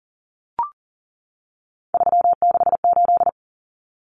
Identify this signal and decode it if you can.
Morse code, heard as a keyed tone, about 40 wpm, 700 Hz. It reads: 368
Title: Numbers - 3-Digit 40wpm